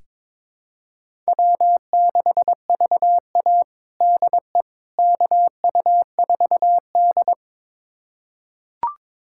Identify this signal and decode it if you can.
Morse code, heard as a keyed tone, about 22 wpm, 700 Hz. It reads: W6VA DE KU4D
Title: USA Callsign DE Callsign - 4 Characters - 22wpm